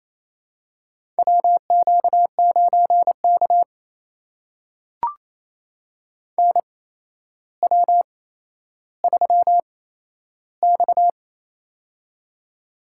Morse code, heard as a keyed tone, about 28 wpm, spaced as for 12 wpm, 700 Hz. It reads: WQ9K NW3X
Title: USA Call Signs - 4 Characters - 8x Character Spacing - 28wpm